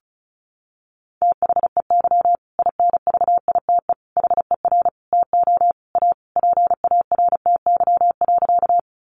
Morse code, heard as a keyed tone, about 34 wpm, 700 Hz. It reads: THEY INVITE HER TO A PARTY.
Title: Easy Phrases 35wpm